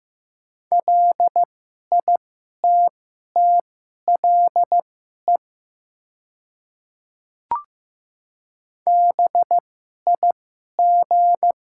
Morse code, heard as a keyed tone, about 15 wpm, spaced as for 11 wpm, 700 Hz. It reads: LITTLE BIG
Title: 2x Character Spacing - Top 200 Words - 15wpm